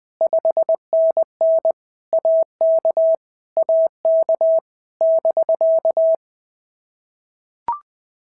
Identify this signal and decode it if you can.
Morse code, heard as a keyed tone, about 20 wpm, 650 Hz. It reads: 5NN AK AK <BK>
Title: POTA QSO Practice - 20wpm